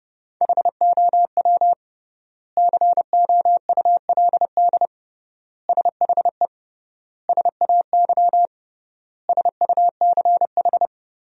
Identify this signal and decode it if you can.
Morse code, heard as a keyed tone, about 30 wpm, 700 Hz. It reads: HOW COULD SHE SAY SUCH
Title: 3x Word-Spacing - Sentences from Top 200 Words - 30wpm